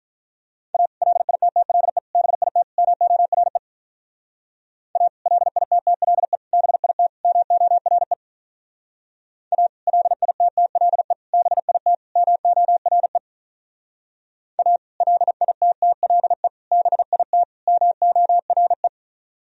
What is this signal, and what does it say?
Morse code, heard as a keyed tone, about 45 wpm, 700 Hz. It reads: A LITTLE BIT MORE A LITTLE BIT MORE A LITTLE BIT MORE A LITTLE BIT MORE